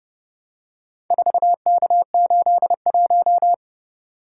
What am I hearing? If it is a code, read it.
Morse code, heard as a keyed tone, about 31 wpm, 700 Hz. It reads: 4K81